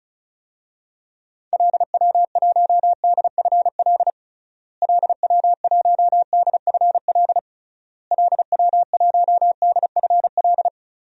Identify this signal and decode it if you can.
Morse code, heard as a keyed tone, about 36 wpm, 700 Hz. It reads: LW1DFL LW1DFL LW1DFL
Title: DXCC-100 Call Signs - 6 Characters - Repeated 3x - 35wpm